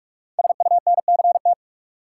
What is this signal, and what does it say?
Morse code, heard as a keyed tone, about 44 wpm, 700 Hz. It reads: SUNCT
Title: Code Groups - 5 Letter:Numbers - 45wpm